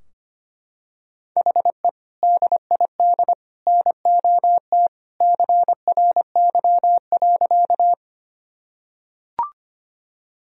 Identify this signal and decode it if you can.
Morse code, heard as a keyed tone, about 25 wpm, 700 Hz. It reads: HE DID NOT CRY.
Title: Sentences from Top 1000 Words - Review Part 2 - 25wpm